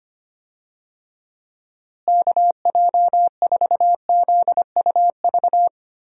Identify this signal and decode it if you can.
Morse code, heard as a keyed tone, about 25 wpm, 700 Hz. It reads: KJ4ZUV
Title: USA Call Signs - 6 Characters - Repeated 2x - 25wpm